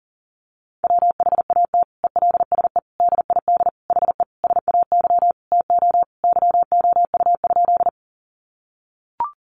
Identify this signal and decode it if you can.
Morse code, heard as a keyed tone, about 40 wpm, 700 Hz. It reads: WHAT ELSE DID HE SAY TO YOU?
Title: Sentences from Top 800 Words 40wpm